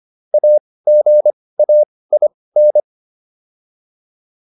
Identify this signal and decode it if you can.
Morse code, heard as a keyed tone, about 25 wpm, spaced as for 18 wpm, 600 Hz. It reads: AGAIN